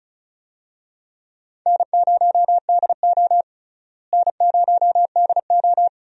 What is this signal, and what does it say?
Morse code, heard as a keyed tone, about 34 wpm, 700 Hz. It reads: N0DO N0DO
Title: USA Call Signs - 4 Characters - Repeated 2x - 35wpm